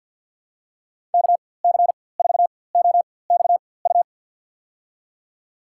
Morse code, heard as a keyed tone, about 53 wpm, spaced as for 24 wpm, 700 Hz. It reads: KC4OXU
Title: USA Call Signs - 6 Characters - 4x Character Spacing - 50wpm